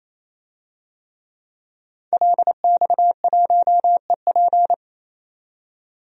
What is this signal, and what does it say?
Morse code, heard as a keyed tone, about 28 wpm, 700 Hz. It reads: LX1EP
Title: DXCC-100 Call Signs - 5 Characters - 28wpm